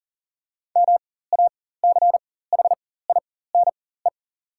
Morse code, heard as a keyed tone, about 40 wpm, spaced as for 19 wpm, 700 Hz. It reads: MACHINE
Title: 4x Character Spacing - Top 500 Words - 40wpm